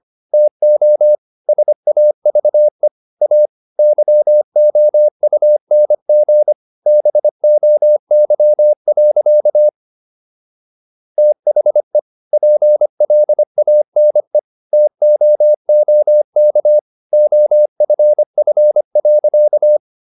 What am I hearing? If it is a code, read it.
Morse code, heard as a keyed tone, about 25 wpm, 600 Hz. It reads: TO SAVE A YOUNG BOY. THE PLANE TOOK OFF.